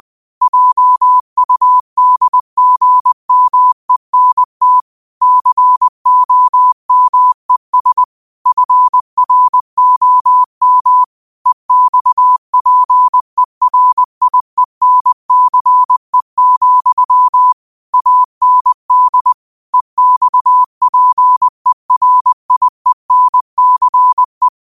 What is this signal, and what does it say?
Morse code, heard as a keyed tone, about 20 wpm, 1000 Hz. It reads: JUDGMENT COMES FROM EXPERIENCE, AND EXPERIENCE